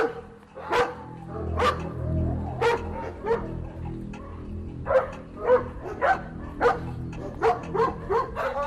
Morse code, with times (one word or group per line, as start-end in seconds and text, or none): none